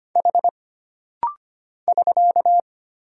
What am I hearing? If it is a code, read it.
Morse code, heard as a keyed tone, about 25 wpm, 700 Hz. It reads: H <SK>
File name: Single Character (Letter:Number:Punctuation:Prosign) - Rapid-Fire - 25wpm